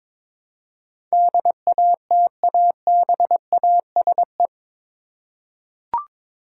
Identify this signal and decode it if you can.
Morse code, heard as a keyed tone, about 22 wpm, 700 Hz. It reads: DATABASE
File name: practice / 22wpm / Eight Letter Words - 22wpm